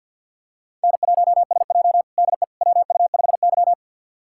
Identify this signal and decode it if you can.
Morse code, heard as a keyed tone, about 46 wpm, 700 Hz. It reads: N1SJ DE WU5Y
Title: USA Callsign DE Callsign - 4 Characters - 50wpm